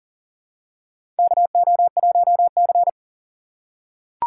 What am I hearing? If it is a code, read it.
Morse code, heard as a keyed tone, about 40 wpm, 700 Hz. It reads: KO1C
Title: USA Call Signs - 4 Characters - 3x Character Spacing - 40wpm